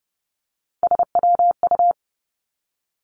Morse code, heard as a keyed tone, about 30 wpm, 700 Hz. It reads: SWU
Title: Three Challenging Letter-Number Combinations 30wpm